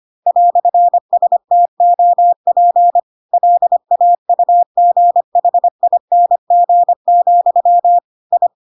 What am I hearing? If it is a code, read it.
Morse code, heard as a keyed tone, about 25 wpm, 700 Hz. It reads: "STOP LAUGHING, I
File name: Pinocchio / Pinocchio ch3 25 25 ik1hge